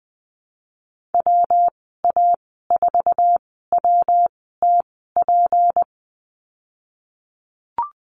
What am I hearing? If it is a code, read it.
Morse code, heard as a keyed tone, about 20 wpm, spaced as for 15 wpm, 700 Hz. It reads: WA4WTP
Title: 231 Lesson 37g - Letter x - Call Signs 20wpm